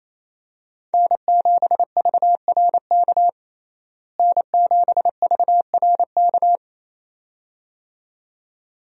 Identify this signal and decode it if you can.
Morse code, heard as a keyed tone, about 28 wpm, 700 Hz. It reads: N7VRK N7VRK
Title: USA Call Signs - 5 Characters - Repeated 2x - 28wpm